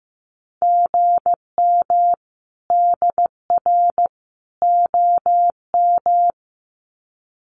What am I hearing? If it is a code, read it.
Morse code, heard as a keyed tone, about 15 wpm, 700 Hz. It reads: GM DR OM
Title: Common QSO Phrases 15wpm